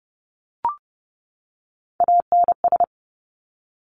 Morse code, heard as a keyed tone, about 30 wpm, 700 Hz. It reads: ANS
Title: Top Three Letter Combinations in Words 30wpm